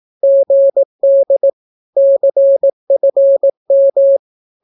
Morse code, heard as a keyed tone, about 18 wpm, 550 Hz. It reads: GD CFM
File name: naukapodstaw / skroty / skroty